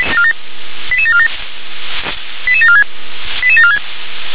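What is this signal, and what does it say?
Five-tone selective call (ZVEI), heard as quick runs of five tones, tones of about 70 ms. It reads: 88747 88746 88746
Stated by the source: ZVEI five-tone selective call 88640 received by SDR.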